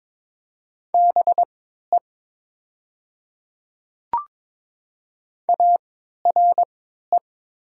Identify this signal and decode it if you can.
Morse code, heard as a keyed tone, about 22 wpm, spaced as for 13 wpm, 700 Hz. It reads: BE ARE
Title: 3x Character Spacing - Top 100 Words - 22wpm